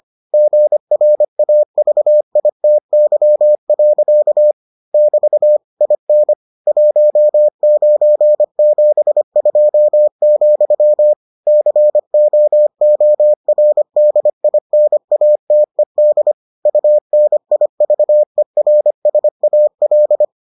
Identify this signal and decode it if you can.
Morse code, heard as a keyed tone, about 25 wpm, 600 Hz. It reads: GRAVITY. <BT> IN 1972, COORDINATED UNIVERSAL